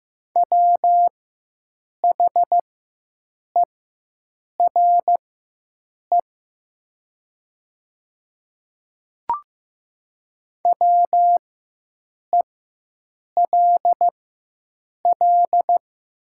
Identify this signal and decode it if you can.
Morse code, heard as a keyed tone, about 15 wpm, spaced as for 7 wpm, 700 Hz. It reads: WHERE WELL
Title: 4x Character Spacing - Top 200 Words - 15wpm